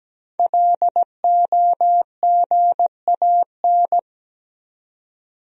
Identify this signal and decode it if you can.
Morse code, heard as a keyed tone, about 17 wpm, 700 Hz. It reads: LOGAN
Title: USA Common Names - 17wpm